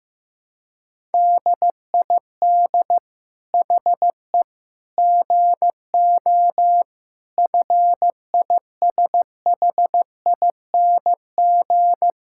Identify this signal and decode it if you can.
Morse code, heard as a keyed tone, about 16 wpm, 700 Hz. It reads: DID HE GO FISHING
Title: Sentences from Top 1000 Words - Review Part 5 - 15wpm